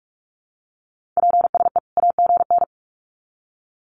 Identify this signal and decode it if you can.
Morse code, heard as a keyed tone, about 44 wpm, 700 Hz. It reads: PSE AGN
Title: Common QSO Phrases 45wpm